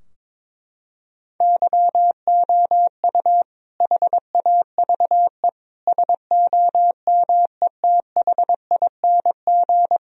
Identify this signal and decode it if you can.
Morse code, heard as a keyed tone, about 22 wpm, 700 Hz. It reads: YOU HAVE SOMETHING
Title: Sentences from Top 1000 Words - Review Part 4 - 22wpm